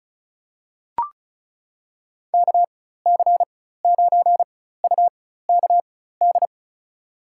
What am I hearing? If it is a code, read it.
Morse code, heard as a keyed tone, about 36 wpm, spaced as for 16 wpm, 700 Hz. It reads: KC9UKD
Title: USA Call Signs - 4x Character Spacing - 35wpm